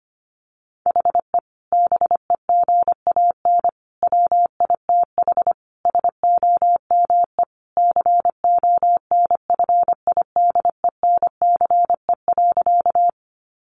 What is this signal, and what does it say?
Morse code, heard as a keyed tone, about 25 wpm, 700 Hz. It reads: HE BEGAN WITH SOME CONFIDENCE.
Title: Sentences from Top 1500 Words 25wpm